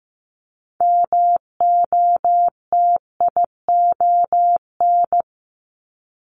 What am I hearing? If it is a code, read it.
Morse code, heard as a keyed tone, about 15 wpm, 700 Hz. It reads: MOTION
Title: Multi-letter Phonemes 15wpm